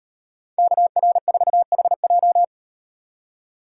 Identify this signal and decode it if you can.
Morse code, heard as a keyed tone, about 36 wpm, 700 Hz. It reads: KR4HJ